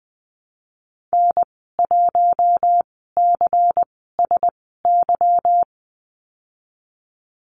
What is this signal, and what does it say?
Morse code, heard as a keyed tone, about 20 wpm, spaced as for 15 wpm, 700 Hz. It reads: N1CSY
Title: 179 Lesson 29g - Slash - Call Signs 20wpm